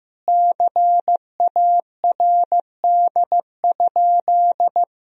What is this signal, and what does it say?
Morse code, heard as a keyed tone, about 15 wpm, 700 Hz. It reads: CARD?
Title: Sentences from Top 1000 Words - Review Part 4 - 15wpm